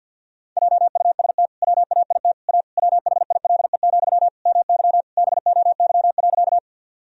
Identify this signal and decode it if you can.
Morse code, heard as a keyed tone, about 47 wpm, 700 Hz. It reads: JUST WAIT A WHILE, MY BOY.
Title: Sentences from Top 1000 Words - Review Part 1 - 50wpm